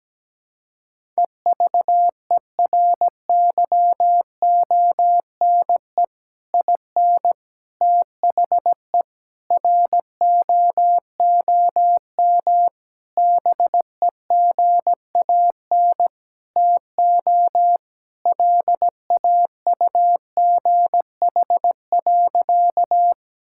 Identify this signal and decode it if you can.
Morse code, heard as a keyed tone, about 17 wpm, 700 Hz. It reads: EVERYONE IN THE ROOM BEGAN TO LAUGH.
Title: Sentences from Top 1000 Words - Review Part 5 - 17wpm